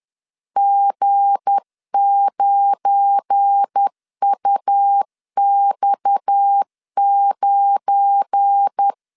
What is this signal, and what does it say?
Morse code, heard as a keyed tone, about 10 wpm, 800 Hz. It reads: G9UX9
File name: LEZIONI / 7 GF9Z / 7 GF9 10wpm 2747